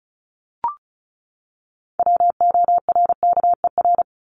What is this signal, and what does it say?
Morse code, heard as a keyed tone, about 34 wpm, 700 Hz. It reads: WORKER
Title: ES:S Suffix - Top 100 Words with Base Words 35wpm